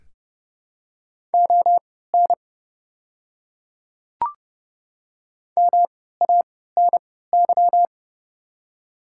Morse code, heard as a keyed tone, about 31 wpm, spaced as for 17 wpm, 700 Hz. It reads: ON MANY
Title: 3x Character Spacing - Top 100 Words - 30wpm